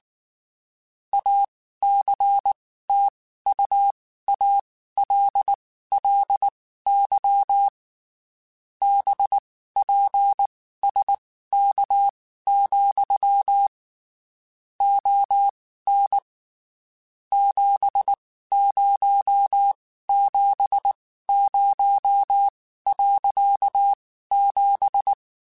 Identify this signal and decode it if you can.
Morse code, heard as a keyed tone, about 19 wpm, spaced as for 14 wpm, 800 Hz. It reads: ACTUALLY BPSK, ON 7070.7
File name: (spaced) / QSOs Spaced-mp3s / kc7sac @18 sp